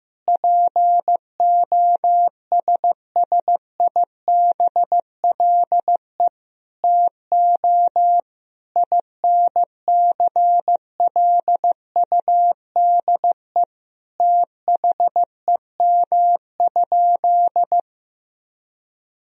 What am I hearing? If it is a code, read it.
Morse code, heard as a keyed tone, about 15 wpm, 700 Hz. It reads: POSSIBLE TO INCLUDE THEM?
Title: Sentences from Top 1000 Words - Review Part 3 - 15wpm